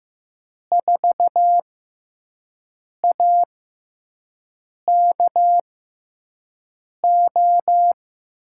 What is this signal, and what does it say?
Morse code, heard as a keyed tone, about 15 wpm, spaced as for 5 wpm, 700 Hz. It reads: 4AKO
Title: USA Call Signs - 6x Character Spacing - 15wpm